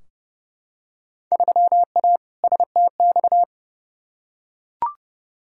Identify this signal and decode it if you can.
Morse code, heard as a keyed tone, about 30 wpm, 700 Hz. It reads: 3A STX
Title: Field Day Exchanges - 30wpm